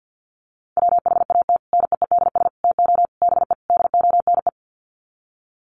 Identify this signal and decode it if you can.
Morse code, heard as a keyed tone, about 50 wpm, 700 Hz. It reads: WHAT NEEDS TO BE DONE
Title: Sets of 5 Words Encore 50wpm